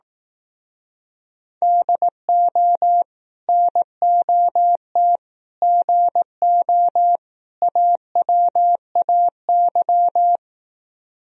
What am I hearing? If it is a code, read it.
Morse code, heard as a keyed tone, about 18 wpm, 700 Hz. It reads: DO NOT GO AWAY
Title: Speed-Racing - Sets of 4 Words 15wpm